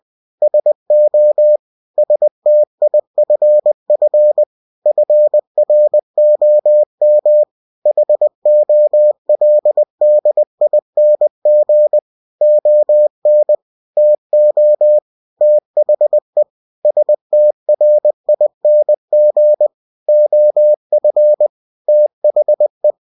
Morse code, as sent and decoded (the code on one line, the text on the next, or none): ... --- / ... - .. ..-. ..-. / ..-. .-. --- -- / .... --- .-.. -.. .. -. --. / --- -. / - --- / - .... / ... - .-. .. -. --. / --- ..-. / - .... .
SO STIFF FROM HOLDING ON TO THE STRING OF THE